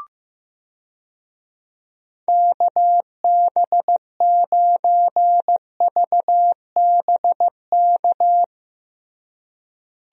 Morse code, heard as a keyed tone, about 15 wpm, 700 Hz. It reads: KB9VBK
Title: USA Call Signs - 6 Characters - Repeated 2x - 15wpm